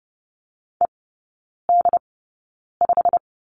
Morse code, heard as a keyed tone, about 30 wpm, spaced as for 9 wpm, 700 Hz. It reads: ED5
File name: course / 30:10wpm / 087 Lesson 15e - Number 5 - Three Characters 30wpm